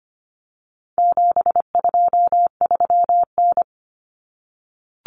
Morse code, heard as a keyed tone, about 25 wpm, 700 Hz. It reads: 723N
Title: Common Cut Numbers - 4-Digit 25wpm